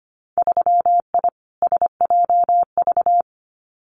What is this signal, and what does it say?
Morse code, heard as a keyed tone, about 25 wpm, 700 Hz. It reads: 3I SJV
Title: Winter Field Day Exchanges Encore 25wpm